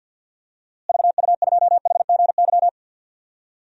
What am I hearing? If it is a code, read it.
Morse code, heard as a keyed tone, about 53 wpm, 700 Hz. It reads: VU2HGY
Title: DXCC-100 Call Signs - 6 Characters - Repeated 3x - 50wpm